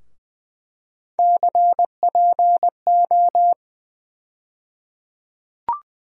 Morse code, heard as a keyed tone, about 20 wpm, 700 Hz. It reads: CPO
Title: USA Call Sign Postfixes - 3 Letters 20wpm